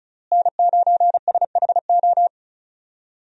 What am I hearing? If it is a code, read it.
Morse code, heard as a keyed tone, about 35 wpm, 700 Hz. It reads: N9SHO
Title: USA Call Signs - 5 Characters - Repeated 2x - 35wpm